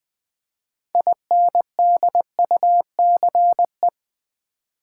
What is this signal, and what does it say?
Morse code, heard as a keyed tone, about 20 wpm, 700 Hz. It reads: INDUCE
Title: 6 Letter Words - First 3 Letters Given - 20wpm